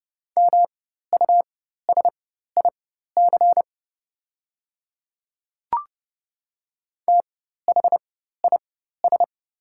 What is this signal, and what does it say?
Morse code, heard as a keyed tone, about 30 wpm, spaced as for 14 wpm, 700 Hz. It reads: MUSIC THIS
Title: 4x Character Spacing - Top 300 Words - 30wpm